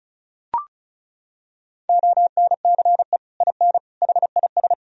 Morse code, heard as a keyed tone, about 34 wpm, 700 Hz. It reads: ONCE IN HIS
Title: Sentences from Top 1000 Words - Review Part 2 - 35wpm